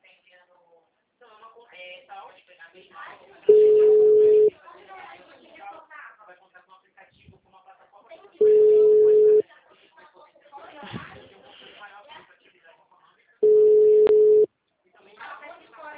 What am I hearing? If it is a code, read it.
Ringback tone (Europe, 425 Hz), with speech.
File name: out- -2009- - - .19841